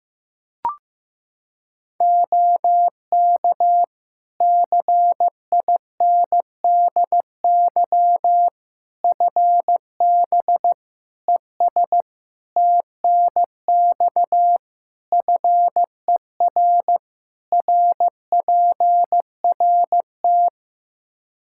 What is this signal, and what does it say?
Morse code, heard as a keyed tone, about 15 wpm, 700 Hz. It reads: OK CINDY FB ES TNX FER RPRT
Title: Practice QSO - Exchange 2 - Info - 15wpm